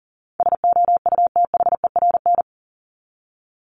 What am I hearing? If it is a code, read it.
Morse code, heard as a keyed tone, about 39 wpm, 700 Hz. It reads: SOUTHERN
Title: Top 1751-2000 Words 40wpm